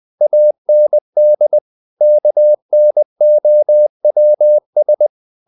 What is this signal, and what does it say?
Morse code, heard as a keyed tone, about 20 wpm, 600 Hz. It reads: AND KNOWS
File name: the old man and the sea part 2 20wpm